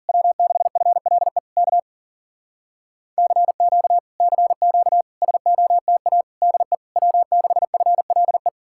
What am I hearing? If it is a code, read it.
Morse code, heard as a keyed tone, about 51 wpm, 700 Hz. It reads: W6FLE K CQ CQ SOTA DE W6FLE